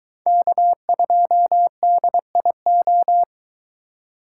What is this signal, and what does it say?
Morse code, heard as a keyed tone, about 23 wpm, 700 Hz. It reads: K2DIO